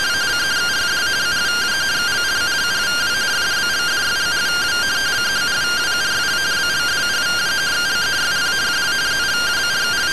Two-tone teleprinter signal, about 170 Hz shift.